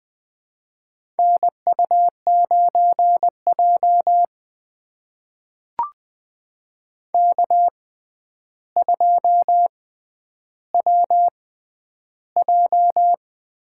Morse code, heard as a keyed tone, about 20 wpm, spaced as for 10 wpm, 700 Hz. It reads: NU9J K2WJ
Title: USA Call Signs - 4 Characters - 6x Character Spacing - 20wpm